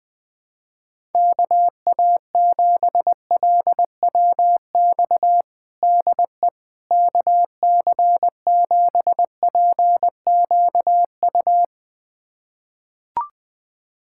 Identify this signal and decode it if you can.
Morse code, heard as a keyed tone, about 20 wpm, 700 Hz. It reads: KA7LWX DE KC7PQU
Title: USA Callsign DE Callsign - 6 Characters - 20wpm